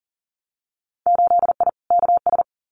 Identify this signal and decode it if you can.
Morse code, heard as a keyed tone, about 40 wpm, 700 Hz. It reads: 8I KS
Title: Winter Field Day Exchanges Encore 40wpm